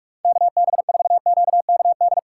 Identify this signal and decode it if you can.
Morse code, heard as a keyed tone, about 41 wpm, 700 Hz. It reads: KB4QKD
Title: USA Callsign DE Callsign - 6 Characters - 45wpm